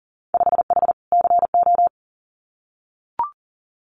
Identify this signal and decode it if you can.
Morse code, heard as a keyed tone, about 38 wpm, 700 Hz. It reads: 5H CO
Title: Winter Field Day Exchanges Encore 40wpm